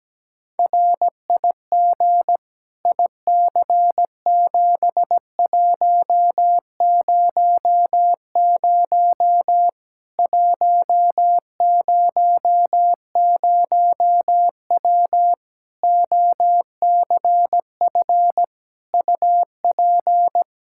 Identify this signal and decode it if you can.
Morse code, heard as a keyed tone, about 17 wpm, 700 Hz. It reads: RIG IC7100 100W OCF UP
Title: Practice QSO - Exchange 2 - Info - 17wpm